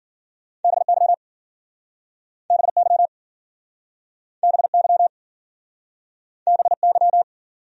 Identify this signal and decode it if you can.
Morse code, heard as a keyed tone, about 52 wpm, 700 Hz. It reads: BY BY BY BY